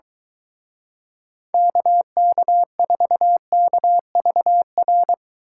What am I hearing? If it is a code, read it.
Morse code, heard as a keyed tone, about 23 wpm, 700 Hz. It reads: KK4KVR